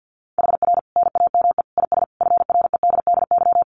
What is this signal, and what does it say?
Morse code, heard as a keyed tone, about 50 wpm, 700 Hz. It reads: HR NAME IS FREDDY
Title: Common QSO Phrases 50wpm